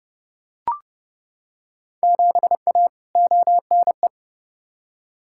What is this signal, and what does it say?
Morse code, heard as a keyed tone, about 30 wpm, 700 Hz. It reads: 7A ONE
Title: Field Day Exchanges Encore - 30wpm